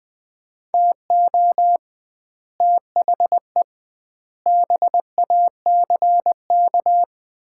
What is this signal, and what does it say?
Morse code, heard as a keyed tone, about 21 wpm, 700 Hz. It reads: TO THE BACK
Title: 2x Word-Spacing - Sets of 5 Words - 20wpm